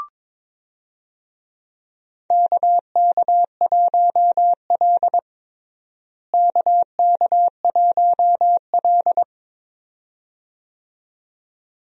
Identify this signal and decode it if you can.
Morse code, heard as a keyed tone, about 22 wpm, 700 Hz. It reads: KK1L KK1L
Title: USA Call Signs - 4 Characters - Repeated 2x - 22wpm